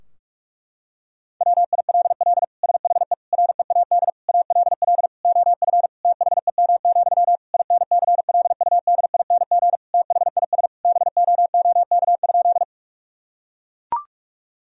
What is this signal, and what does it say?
Morse code, heard as a keyed tone, about 41 wpm, 700 Hz. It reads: WILL SHE READ ALL OF THEM, INCLUDING THIS BOOK?
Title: Sentences from Top 1000 Words - Review Part 4 - 45wpm